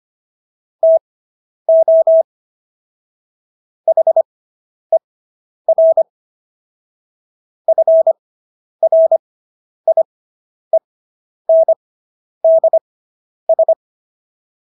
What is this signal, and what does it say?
Morse code, heard as a keyed tone, about 25 wpm, spaced as for 10 wpm, 650 Hz. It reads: TO HER FRIENDS